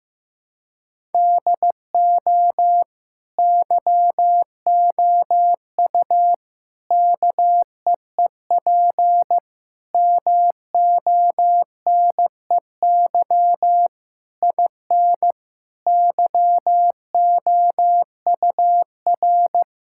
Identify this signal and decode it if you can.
Morse code, heard as a keyed tone, about 16 wpm, 700 Hz. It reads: DO YOU KEEP MONEY IN YOUR
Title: Sentences from Top 1000 Words - Review Part 6 - 15wpm